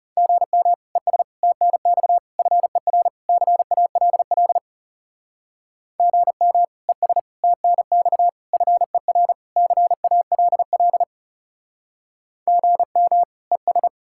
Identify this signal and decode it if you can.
Morse code, heard as a keyed tone, about 36 wpm, 700 Hz. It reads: GM ES TNX FER CALL GM ES TNX FER CALL GM ES
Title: Speed-Racing Common QSO Phrases - 30wpm